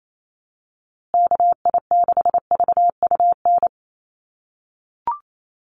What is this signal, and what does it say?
Morse code, heard as a keyed tone, about 28 wpm, 700 Hz. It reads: KI6VUN
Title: USA Call Signs Repeated 2x 28wpm